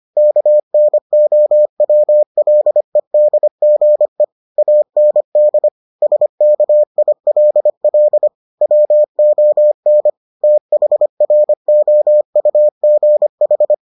Morse code, heard as a keyed tone, about 25 wpm, 600 Hz. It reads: KNOWLEDGE AND SKILL WON THROUGH